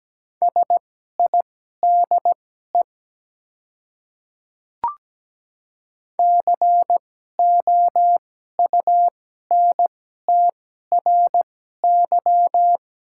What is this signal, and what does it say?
Morse code, heard as a keyed tone, about 17 wpm, spaced as for 12 wpm, 700 Hz. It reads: SIDE COUNTRY
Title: 2x Character Spacing - Top 300 Words - 17wpm